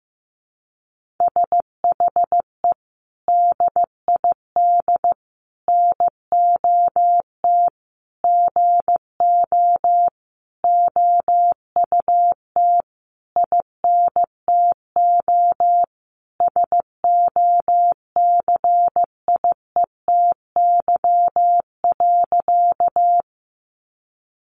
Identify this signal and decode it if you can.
Morse code, heard as a keyed tone, about 15 wpm, 700 Hz. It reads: SHE DID NOT GO OUT INTO SOCIETY.
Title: Sentences from Top 700 Words 15wpm